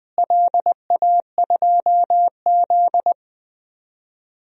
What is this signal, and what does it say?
Morse code, heard as a keyed tone, about 21 wpm, 700 Hz. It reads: LA2Z
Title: DXCC-100 Call Signs - 4 Characters - Repeated 3x - 20wpm